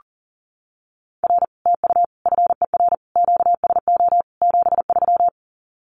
Morse code, heard as a keyed tone, about 40 wpm, 700 Hz. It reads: R TU FER QSO 73
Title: Common QSO Phrases Encore 40wpm